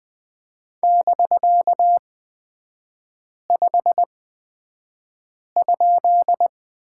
Morse code, heard as a keyed tone, about 20 wpm, spaced as for 5 wpm, 700 Hz. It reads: <BK>5?
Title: 250 Lesson 40e - BK - Three Characters 20wpm